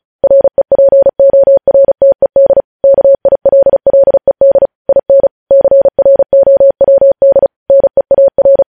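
Morse code, heard as a keyed tone, about 36 wpm, 550 Hz. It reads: REPORTED KILLED IN CROWD NEAR
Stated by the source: News headlines sent in Morse at speed 35.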